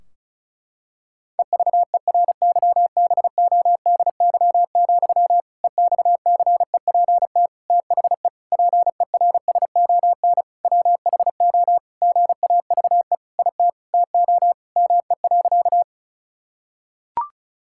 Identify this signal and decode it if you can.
Morse code, heard as a keyed tone, about 36 wpm, 700 Hz. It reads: EVERYBODY, EXCEPT THE PERSON WHO GAVE IT TO ME.
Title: Sentences from Top 1600 Words 35wpm